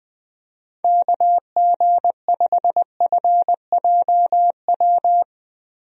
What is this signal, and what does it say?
Morse code, heard as a keyed tone, about 20 wpm, 700 Hz. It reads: KG5FJW
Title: USA Call Signs - 6 Characters - Repeated 2x - 20wpm